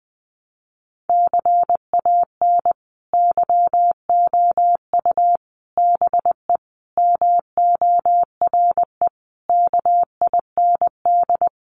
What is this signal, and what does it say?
Morse code, heard as a keyed tone, about 19 wpm, 700 Hz. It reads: CAN YOU BE MORE KIND
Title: Sentences from Top 1100 Words 20wpm